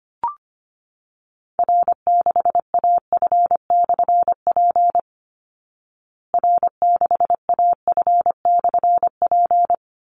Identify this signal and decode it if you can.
Morse code, heard as a keyed tone, about 25 wpm, 700 Hz. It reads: R6AF/P R6AF/P
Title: Call Signs Repeated 3x - 25wpm